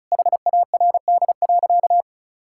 Morse code, heard as a keyed tone, about 34 wpm, 700 Hz. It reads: HARD.
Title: Pangram Challenge v2 - 35wpm